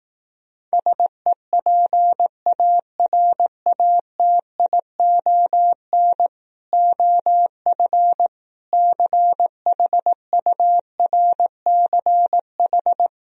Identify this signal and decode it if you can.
Morse code, heard as a keyed tone, about 18 wpm, 700 Hz. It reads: SEPARATION OF CHURCH
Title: Speed-Racing - Sets of 5 Words 15wpm